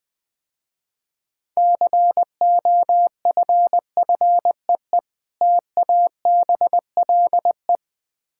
Morse code, heard as a keyed tone, about 20 wpm, 700 Hz. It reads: COFFEE TABLE
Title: Difficult Words - Speed-Racing with Hints - 20wpm